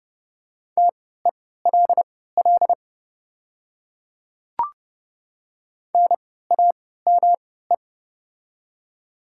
Morse code, heard as a keyed tone, about 30 wpm, spaced as for 17 wpm, 700 Hz. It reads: TELL NAME
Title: 3x Character Spacing - Top 200 Words - 30wpm